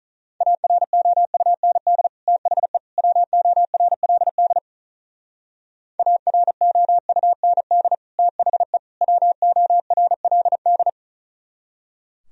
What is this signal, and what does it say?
Morse code, heard as a keyed tone, about 44 wpm, 700 Hz. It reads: AROUND THE WORLD AROUND THE WORLD
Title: Speed-Racing - Sets of 3 Words 35wpm